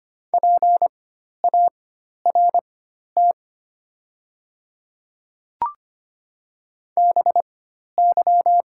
Morse code, heard as a keyed tone, about 25 wpm, spaced as for 12 wpm, 700 Hz. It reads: PART BY
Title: 4x Character Spacing - Top 200 Words - 25wpm